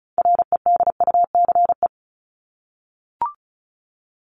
Morse code, heard as a keyed tone, about 34 wpm, 700 Hz. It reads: REDUCE
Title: Top 1001-1250 Words 35wpm